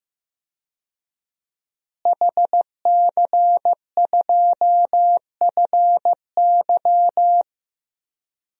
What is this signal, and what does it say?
Morse code, heard as a keyed tone, about 15 wpm, 700 Hz. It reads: HC2FY
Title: DXCC-100 Call Signs - 5 Characters - Repeated 2x - 15wpm